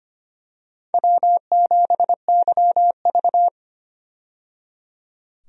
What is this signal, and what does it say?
Morse code, heard as a keyed tone, about 25 wpm, 700 Hz. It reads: W7YV
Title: USA Call Signs - 4 Characters - 25wpm